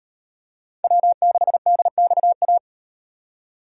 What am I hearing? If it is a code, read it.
Morse code, heard as a keyed tone, about 36 wpm, 700 Hz. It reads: W6DXA